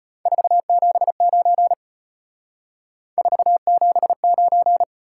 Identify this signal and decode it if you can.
Morse code, heard as a keyed tone, about 31 wpm, 700 Hz. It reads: 479 479
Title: Speed-Racing Top 180 QSO Elements 25wpm